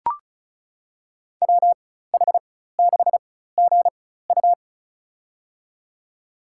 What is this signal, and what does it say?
Morse code, heard as a keyed tone, about 36 wpm, spaced as for 16 wpm, 700 Hz. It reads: WH6GU
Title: USA Call Signs - 5 Characters - 4x Character Spacing - 35wpm